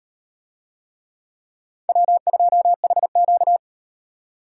Morse code, heard as a keyed tone, about 36 wpm, 700 Hz. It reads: W2HQ